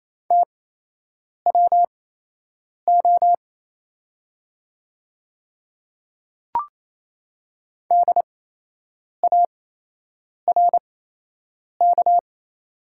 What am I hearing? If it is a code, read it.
Morse code, heard as a keyed tone, about 28 wpm, spaced as for 8 wpm, 700 Hz. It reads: TWO DARK
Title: 8x Character Spacing - Top 500 Words - 28wpm